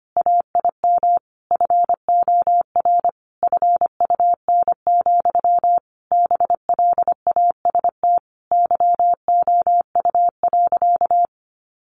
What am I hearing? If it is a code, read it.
Morse code, heard as a keyed tone, about 25 wpm, 700 Hz. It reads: AIM FOR FUN, BLAST YOU.
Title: A Sound Of Thunder 25wpm